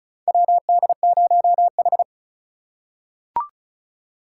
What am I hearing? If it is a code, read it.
Morse code, heard as a keyed tone, about 36 wpm, 700 Hz. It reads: WD0H
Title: USA Call Signs - 4 Characters - 4x Character Spacing - 35wpm